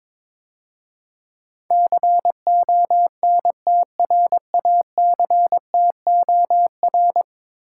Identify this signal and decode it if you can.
Morse code, heard as a keyed tone, about 22 wpm, 700 Hz. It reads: CONTRACTOR
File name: practice / 22wpm / 10 Letter Words - First 3 Letters Given - 22wpm